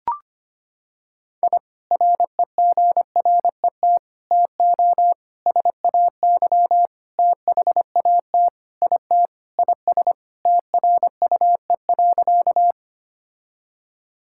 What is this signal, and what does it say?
Morse code, heard as a keyed tone, about 25 wpm, 700 Hz. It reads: I REGRET TO SAY THAT IT IS TRUE.
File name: practice / 25wpm / Sentences with 3-Word Phrases - 25wpm